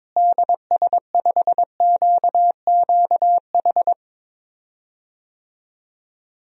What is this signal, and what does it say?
Morse code, heard as a keyed tone, about 22 wpm, 700 Hz. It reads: DS5QQH
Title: DXCC-100 Call Signs - 6 Characters - Repeated 3x - 22wpm